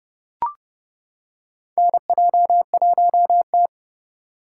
Morse code, heard as a keyed tone, about 32 wpm, 700 Hz. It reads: NJ1T
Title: K1USN SST QSO Practice - 30wpm